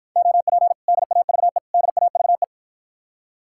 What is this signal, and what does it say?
Morse code, heard as a keyed tone, about 54 wpm, 700 Hz. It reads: OP DAVE DAVE